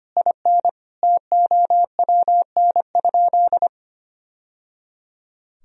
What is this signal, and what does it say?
Morse code, heard as a keyed tone, about 25 wpm, 700 Hz. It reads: IN TOWN?
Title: Sentences from Top 1000 Words - Review Part 4 - 25wpm